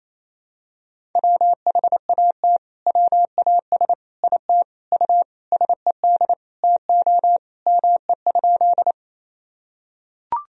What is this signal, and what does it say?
Morse code, heard as a keyed tone, about 28 wpm, 700 Hz. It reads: WHAT WAS IT U SED TO ME?
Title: Sentences from Top 200 Words and Abbreviations 28wpm